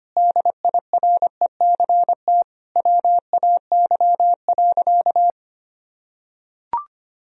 Morse code, heard as a keyed tone, about 25 wpm, 700 Hz. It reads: DIRECT WAY.
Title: Sentences from Top 1000 Words - Review Part 2 - 25wpm